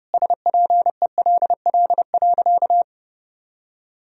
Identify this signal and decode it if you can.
Morse code, heard as a keyed tone, about 30 wpm, 700 Hz. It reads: SPELL.
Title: Sentences from Top 1000 Words - Review Part 1 - 30wpm